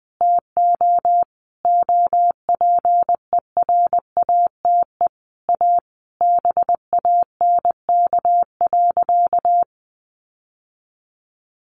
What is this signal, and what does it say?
Morse code, heard as a keyed tone, about 20 wpm, 700 Hz. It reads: TO OPERATE A BANK.
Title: Sentences from Top 1500 Words 20wpm